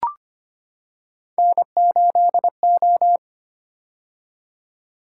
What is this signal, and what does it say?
Morse code, heard as a keyed tone, about 25 wpm, 700 Hz. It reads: N8O
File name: Three Letter-Number Combinations - 25wpm